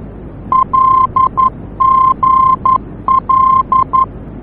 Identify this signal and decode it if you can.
Morse code, heard as a keyed tone, about 11 wpm, 1050 Hz. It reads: LGL
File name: france / mx / LGL-F-VOR